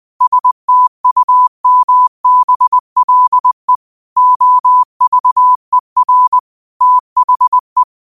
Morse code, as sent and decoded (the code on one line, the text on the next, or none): ... - ..- -- -... .-.. / --- ...- . .-. / - .... .
STUMBLE OVER THE